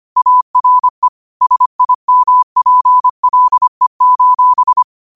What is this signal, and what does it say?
Morse code, heard as a keyed tone, about 25 wpm, 1000 Hz. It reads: ARE SIMPLE:
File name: qotdfiles / 25wpm